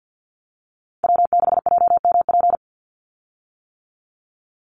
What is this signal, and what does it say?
Morse code, heard as a keyed tone, about 50 wpm, 700 Hz. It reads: W6JMP
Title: Call Signs - 50wpm